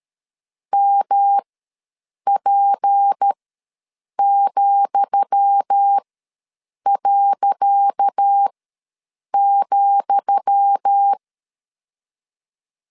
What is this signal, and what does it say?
Morse code, heard as a keyed tone, about 13 wpm, spaced as for 7 wpm, 800 Hz. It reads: MP,.,
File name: LEZIONI / 10 SIGNS / 10D COMA ws 12wpm 3618